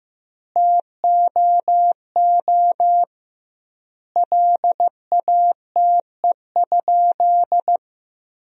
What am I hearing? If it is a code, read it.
Morse code, heard as a keyed tone, about 15 wpm, 700 Hz. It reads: TOO LATE?
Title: 2x Word-Spacing - Sentences from Top 300 Words - 15wpm